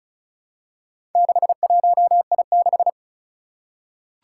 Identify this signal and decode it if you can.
Morse code, heard as a keyed tone, about 36 wpm, 700 Hz. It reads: 61I6
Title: Four Challenging Letter-Number Combinations 35wpm